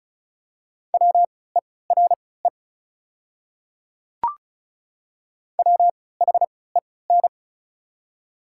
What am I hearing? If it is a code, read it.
Morse code, heard as a keyed tone, about 36 wpm, spaced as for 20 wpm, 700 Hz. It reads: WERE WHEN
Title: 3x Character Spacing - Top 100 Words - 35wpm